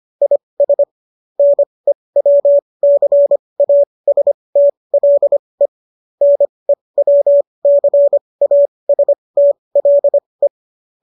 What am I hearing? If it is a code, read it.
Morse code, heard as a keyed tone, about 25 wpm, spaced as for 20 wpm, 570 Hz. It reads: IS NEWCASTLE NEWCASTLE